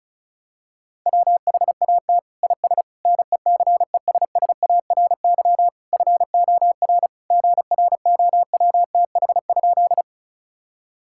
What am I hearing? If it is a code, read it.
Morse code, heard as a keyed tone, about 34 wpm, 700 Hz. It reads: WHAT IS NECESSARY FOR GROWTH?
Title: Sentences from Top 1000 Words - Review Part 5 - 35wpm